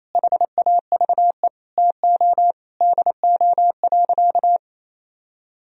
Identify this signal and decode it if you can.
Morse code, heard as a keyed tone, about 28 wpm, 700 Hz. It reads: HAVE TO DO.
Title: Sentences from Top 1000 Words - Review Part 2 - 28wpm